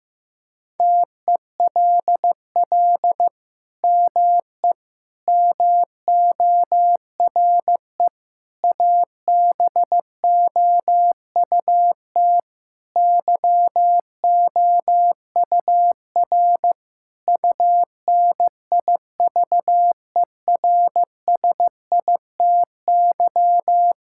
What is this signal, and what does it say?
Morse code, heard as a keyed tone, about 15 wpm, 690 Hz. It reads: TELL ME MORE ABOUT YOUR UNIVERSITY
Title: Sentences from Top 1000 Words - Review Part 5 - 15wpm